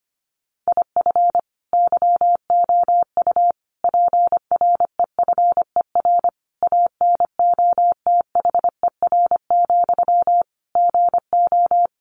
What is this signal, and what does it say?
Morse code, heard as a keyed tone, about 25 wpm, 700 Hz. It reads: IF YOU PREFER ANOTHER, GO
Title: Sentences from Top 1100 Words 25wpm